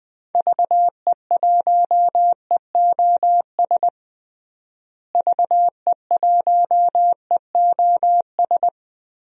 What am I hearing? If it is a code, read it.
Morse code, heard as a keyed tone, about 20 wpm, 700 Hz. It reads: VE1EOS VE1EOS
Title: DXCC-100 Call Signs - 6 Characters - Repeated 3x - 20wpm